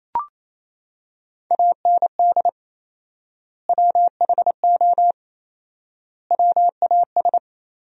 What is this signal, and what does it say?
Morse code, heard as a keyed tone, about 28 wpm, 700 Hz. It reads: AND WHO WAS
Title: 4x Word-Spacing - Sentences from Top 100 Words - 28wpm